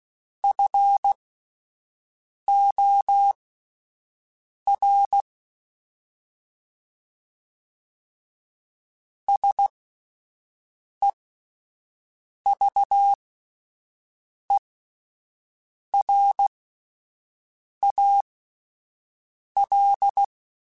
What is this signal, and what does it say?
Morse code, heard as a keyed tone, about 16 wpm, spaced as for 5 wpm, 800 Hz. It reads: FOR SEVERAL
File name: (spaced) / QSOs Spaced-mp3s / WA7QOX 2 @10 sp